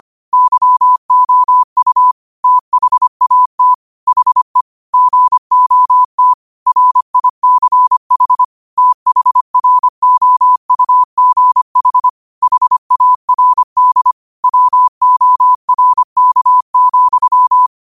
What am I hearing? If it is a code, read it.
Morse code, heard as a keyed tone, about 25 wpm, 1000 Hz. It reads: YOU THAT HE GOT RICH THROUGH HARD WORK,